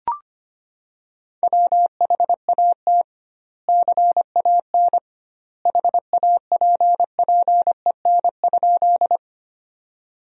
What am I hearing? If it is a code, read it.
Morse code, heard as a keyed tone, about 25 wpm, 700 Hz. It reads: WHAT CAN HAPPEN?
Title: 2x Word-Spacing - Sentences from Top 400 Words - 25wpm